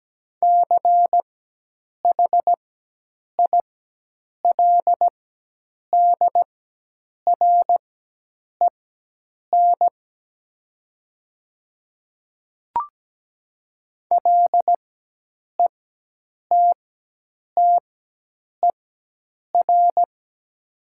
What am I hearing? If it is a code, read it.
Morse code, heard as a keyed tone, about 17 wpm, spaced as for 8 wpm, 700 Hz. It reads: CHILDREN LETTER
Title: 4x Character Spacing - Top 300 Words - 17wpm